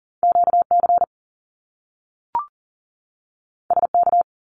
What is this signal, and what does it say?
Morse code, heard as a keyed tone, about 39 wpm, 700 Hz. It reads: QC SK
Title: Canadian Provinces and Territories 40wpm